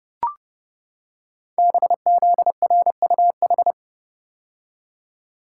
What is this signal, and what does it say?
Morse code, heard as a keyed tone, about 29 wpm, 700 Hz. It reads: BZRUH
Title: Code Groups - 5 Letters - 30wpm